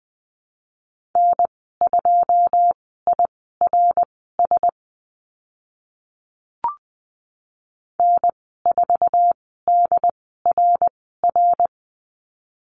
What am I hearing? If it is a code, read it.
Morse code, heard as a keyed tone, about 20 wpm, spaced as for 14 wpm, 700 Hz. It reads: N2IRS N4DRR
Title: 088 Lesson 15g - Number 5 - Call Signs 20wpm